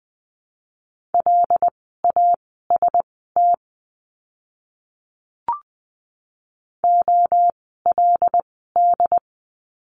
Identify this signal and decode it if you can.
Morse code, heard as a keyed tone, about 20 wpm, spaced as for 14 wpm, 700 Hz. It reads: LAST OLD
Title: 237 Lesson 38f - Letter q - Words 20wpm